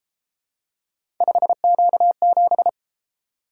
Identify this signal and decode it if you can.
Morse code, heard as a keyed tone, about 30 wpm, 700 Hz. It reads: 5Q7